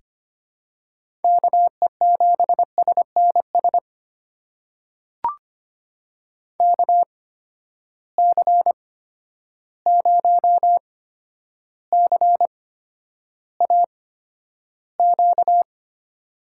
Morse code, heard as a keyed tone, about 25 wpm, spaced as for 11 wpm, 700 Hz. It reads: KE7SNS KC0CAQ
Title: USA Call Signs - 6 Characters - 8x Character Spacing - 25wpm